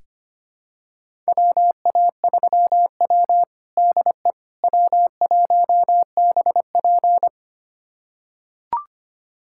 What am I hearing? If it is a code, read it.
Morse code, heard as a keyed tone, about 25 wpm, 700 Hz. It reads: WA3W DE W1BP
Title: USA Callsign DE Callsign - 4 Characters - 25wpm